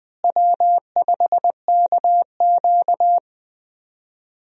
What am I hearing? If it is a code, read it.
Morse code, heard as a keyed tone, about 20 wpm, 700 Hz. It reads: W5KQ
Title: USA Call Signs - 4 Characters - Repeated 3x - 20wpm